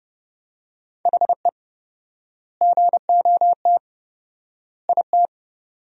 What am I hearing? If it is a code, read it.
Morse code, heard as a keyed tone, about 32 wpm, 700 Hz. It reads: HE GOT IT
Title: 4x Word-Spacing - Sentences from Top 400 Words - 30wpm